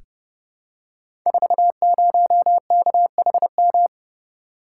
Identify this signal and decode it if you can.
Morse code, heard as a keyed tone, about 32 wpm, 700 Hz. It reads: 40KHM
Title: Code Groups - 5 Letter:Numbers - 30wpm